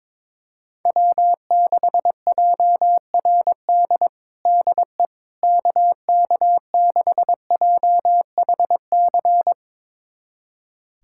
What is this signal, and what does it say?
Morse code, heard as a keyed tone, about 22 wpm, 700 Hz. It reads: W6JRD DE KK6JHC
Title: USA Callsign DE Callsign 22wpm